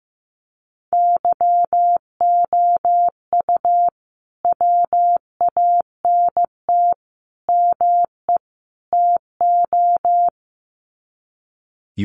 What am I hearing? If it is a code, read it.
Morse code, heard as a keyed tone, about 15 wpm, 700 Hz. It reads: YOU WANT ME TO
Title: Sets of 4 Words 15wpm